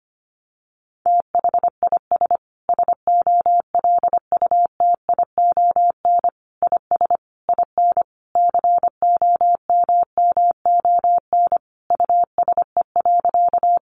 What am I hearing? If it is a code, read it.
Morse code, heard as a keyed tone, about 25 wpm, 700 Hz. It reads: THIS SOLUTION IS IN COMMON USE.
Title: Sentences from Top 1300 Words 25wpm